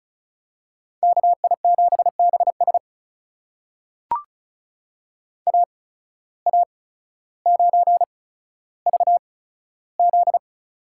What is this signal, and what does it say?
Morse code, heard as a keyed tone, about 36 wpm, spaced as for 15 wpm, 700 Hz. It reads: KI7BS AA9VZ
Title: USA Call Signs - 5 Characters - 8x Character Spacing - 35wpm